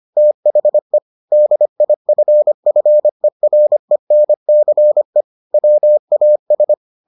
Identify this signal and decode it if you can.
Morse code, heard as a keyed tone, about 25 wpm, 600 Hz. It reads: THE DIFFERENCE WAS